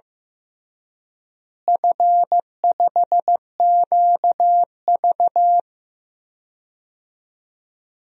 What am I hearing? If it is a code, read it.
Morse code, heard as a keyed tone, about 15 wpm, 700 Hz. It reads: F5QV
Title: DXCC-100 Call Signs - 4 Characters - Repeated 3x - 15wpm